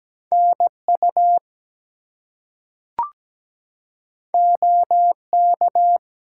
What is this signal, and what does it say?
Morse code, heard as a keyed tone, about 17 wpm, 700 Hz. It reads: NU OK
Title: US States and Canadian Provinces 17wpm